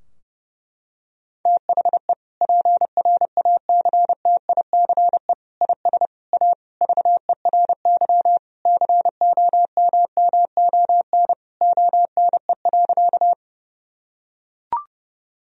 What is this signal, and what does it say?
Morse code, heard as a keyed tone, about 30 wpm, 700 Hz. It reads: THE PRACTICE IS A VERY COMMON ONE.
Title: Sentences from Top 1000 Words - Review Part 4 - 30wpm